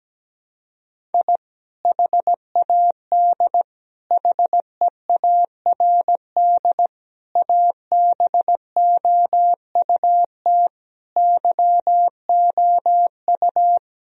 Morse code, heard as a keyed tone, about 17 wpm, 700 Hz. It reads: I HAD HEARD ABOUT YOU
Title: Sentences from Top 1000 Words - Review Part 1 - 17wpm